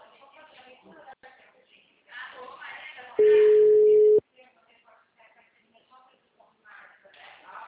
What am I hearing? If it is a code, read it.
Ringback tone (Europe, 425 Hz), with speech.
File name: out- -2003- - - .73998